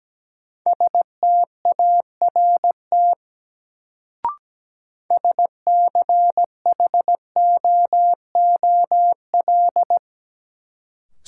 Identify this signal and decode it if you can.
Morse code, heard as a keyed tone, about 17 wpm, 700 Hz. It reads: START SCHOOL
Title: Sentences from Top 300 Words - New Word Drill - 17wpm